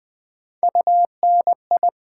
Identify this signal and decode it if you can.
Morse code, heard as a keyed tone, about 20 wpm, 700 Hz. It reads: UNI
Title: Build A Word - 3 Letters At A Time - 6-Letter Words - 20wpm